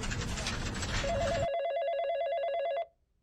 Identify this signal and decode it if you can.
There is no code.